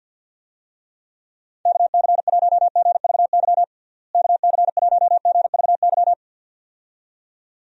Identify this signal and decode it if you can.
Morse code, heard as a keyed tone, about 50 wpm, 700 Hz. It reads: KC1GVY KC1GVY
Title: USA Call Signs - 6 Characters - Repeated 2x - 50wpm